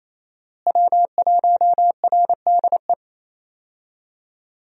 Morse code, heard as a keyed tone, about 28 wpm, 700 Hz. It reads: W1RDE
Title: USA Call Signs - 5 Characters - Repeated 2x - 28wpm